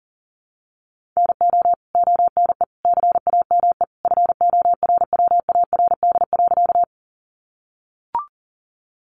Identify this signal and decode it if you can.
Morse code, heard as a keyed tone, about 40 wpm, 700 Hz. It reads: NO ONE CAME FORWARD.
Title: Sentences from Top 1300 Words 40wpm